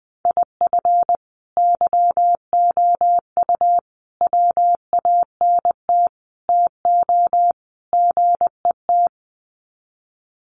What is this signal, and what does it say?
Morse code, heard as a keyed tone, about 20 wpm, 700 Hz. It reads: IF YOU WANT TO GET
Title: Sets of 5 Words Encore 20wpm